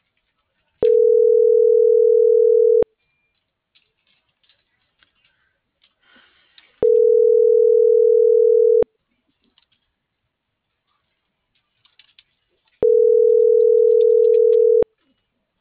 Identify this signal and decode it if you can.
Ringback tone (North America), 440 + 480 Hz.